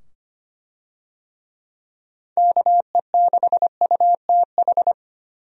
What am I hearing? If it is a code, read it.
Morse code, heard as a keyed tone, about 25 wpm, 700 Hz. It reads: KE6UTH
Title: USA Call Signs - 6 Characters - 25wpm